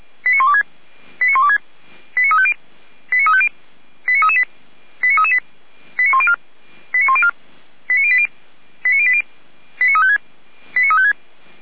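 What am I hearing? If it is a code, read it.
Five-tone selective call (ZVEI), heard as quick runs of five tones, tones of about 70 ms. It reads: 89126 89126 89350 89350 89338 89338 89104 89104 89080 89080 89356 89356
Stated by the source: ZVEI five-tone selective call received by SDR, from the Feuerwehr Orlamünde fire brigade audio server.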